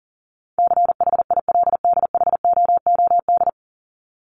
Morse code, heard as a keyed tone, about 40 wpm, 700 Hz. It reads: CHILDHOOD
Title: Top 1751-2000 Words 40wpm